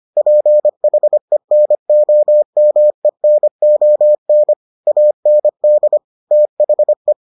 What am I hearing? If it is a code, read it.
Morse code, heard as a keyed tone, about 25 wpm, 600 Hz. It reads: PHENOMENON AND THE